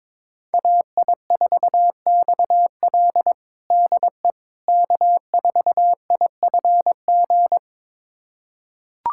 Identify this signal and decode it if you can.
Morse code, heard as a keyed tone, about 22 wpm, 700 Hz. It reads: AI4XL DE K4IFG
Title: USA Callsign DE Callsign - 5 Characters - 22wpm